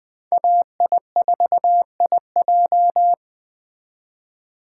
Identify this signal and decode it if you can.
Morse code, heard as a keyed tone, about 20 wpm, 700 Hz. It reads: AI4IJ